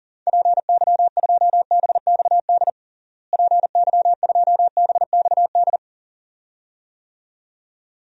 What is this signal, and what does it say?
Morse code, heard as a keyed tone, about 40 wpm, 700 Hz. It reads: PY2BXD PY2BXD
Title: DXCC-100 Call Signs - 6 Characters - Repeated 2x - 40wpm